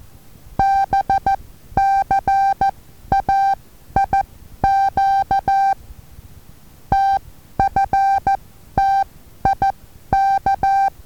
Morse code, heard as a keyed tone, about 14 wpm, spaced as for 11 wpm, 780 Hz. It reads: BCAIQ TFTIK
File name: Taviro HA6VA B